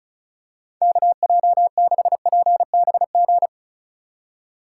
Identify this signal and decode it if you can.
Morse code, heard as a keyed tone, about 36 wpm, 700 Hz. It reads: KJ6PBG
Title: USA Call Signs - 6 Characters - 2x Character Spacing - 35wpm